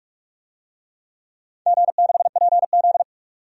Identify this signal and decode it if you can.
Morse code, heard as a keyed tone, about 44 wpm, 700 Hz. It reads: G6PZ
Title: DXCC-100 Call Signs - 4 Characters - 45wpm